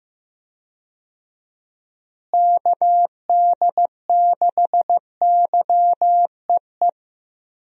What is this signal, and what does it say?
Morse code, heard as a keyed tone, about 15 wpm, 700 Hz. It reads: KD6YEE
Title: USA Call Signs - 6 Characters - Repeated 3x - 15wpm